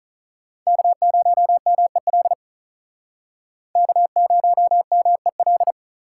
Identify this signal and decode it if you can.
Morse code, heard as a keyed tone, about 44 wpm, 700 Hz. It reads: K0MEL K0MEL